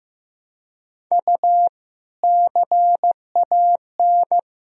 Morse code, heard as a keyed tone, about 15 wpm, 700 Hz. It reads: U CAN
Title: Sentences from Top 200 Words and Abbreviations 15wpm